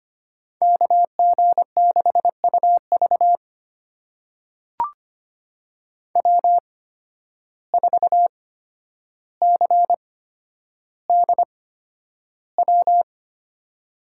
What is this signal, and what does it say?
Morse code, heard as a keyed tone, about 25 wpm, spaced as for 11 wpm, 700 Hz. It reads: KG6UV W4CDW
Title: USA Call Signs - 5 Characters - 8x Character Spacing - 25wpm